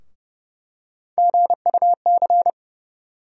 Morse code, heard as a keyed tone, about 30 wpm, 700 Hz. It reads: GUC
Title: USA Call Sign Postfixes - 3 Letters 30wpm